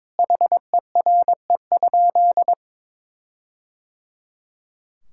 Morse code, heard as a keyed tone, about 22 wpm, 700 Hz. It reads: HERE?
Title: Sentences from Top 1600 Words 22wpm